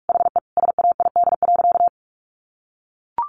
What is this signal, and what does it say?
Morse code, heard as a keyed tone, about 44 wpm, 700 Hz. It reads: HE SAID.
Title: Sentences from Top 1300 Words 45wpm